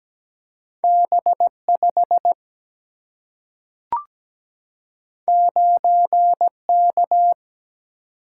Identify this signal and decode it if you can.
Morse code, heard as a keyed tone, about 17 wpm, 700 Hz. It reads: B5 9K
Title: Single-Letter Single-Number 17wpm